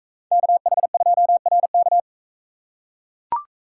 Morse code, heard as a keyed tone, about 42 wpm, 700 Hz. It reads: KH2RK